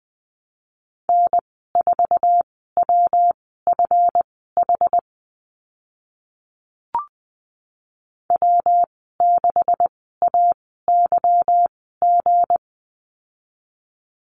Morse code, heard as a keyed tone, about 20 wpm, spaced as for 14 wpm, 700 Hz. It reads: N4WFH W6AYG
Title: 161 Lesson 26g - Letter g - Call Signs 20wpm